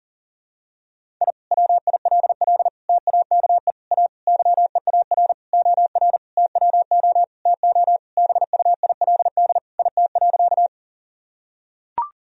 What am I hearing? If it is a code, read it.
Morse code, heard as a keyed tone, about 40 wpm, 700 Hz. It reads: I WILL TAKE A YEAR OR TWO TO BUILD IT.
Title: Sentences from Top 1000 Words - Review Part 1 - 40wpm